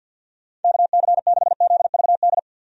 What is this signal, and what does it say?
Morse code, heard as a keyed tone, about 52 wpm, 700 Hz. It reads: KC6ZVD
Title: USA Call Signs - 6 Characters - 50wpm